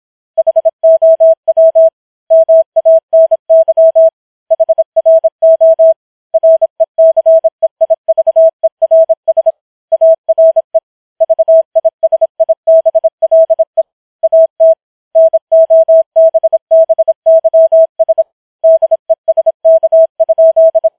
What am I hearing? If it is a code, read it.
Morse code, heard as a keyed tone, about 26 wpm, 650 Hz. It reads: HOW MANY HRO RECEIVERS ARE VISIBLE AT NOBBYS DESK?